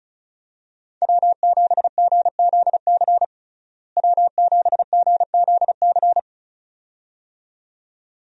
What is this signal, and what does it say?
Morse code, heard as a keyed tone, about 34 wpm, 700 Hz. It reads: W7GZC W7GZC
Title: USA Call Signs - 5 Characters - Repeated 2x - 35wpm